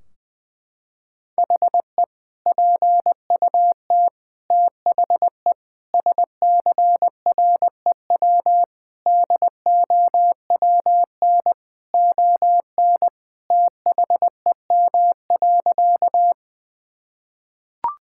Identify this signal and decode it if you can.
Morse code, heard as a keyed tone, about 20 wpm, 700 Hz. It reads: HE PUT THE SCREW DOWN ON THEM.
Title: Sentences from Top 1600 Words 20wpm